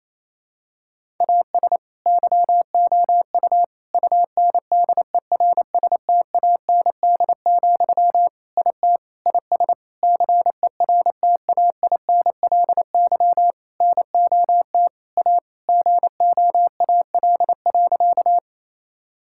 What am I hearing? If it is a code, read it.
Morse code, heard as a keyed tone, about 28 wpm, 700 Hz. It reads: AS YOU UNDERSTAND, IT IS CERTAINLY NOT A GOAL.
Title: Sentences from Top 1000 Words - Review Part 5 - 28wpm